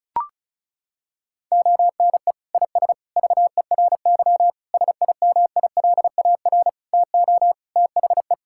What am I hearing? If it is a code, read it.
Morse code, heard as a keyed tone, about 36 wpm, 700 Hz. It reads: ONE IS VERY SIMILAR TO THE
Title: Sentences from Top 1000 Words - Review Part 5 - 35wpm